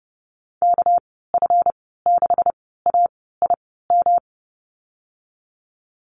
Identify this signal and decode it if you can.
Morse code, heard as a keyed tone, about 30 wpm, spaced as for 17 wpm, 700 Hz. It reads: KF6AIM
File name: Lesson 37g - Letter x - Call Signs 30wpm